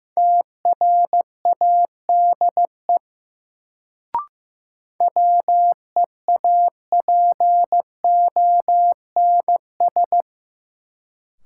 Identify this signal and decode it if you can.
Morse code, heard as a keyed tone, about 15 wpm, 700 Hz. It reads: TRADE WEAPONS
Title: Sets of 5 Words Encore - New Word Drill - 15wpm